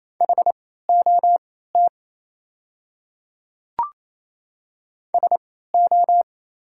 Morse code, heard as a keyed tone, about 28 wpm, spaced as for 16 wpm, 690 Hz. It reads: HOT SO
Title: 3x Character Spacing - Top 100 Words - 28wpm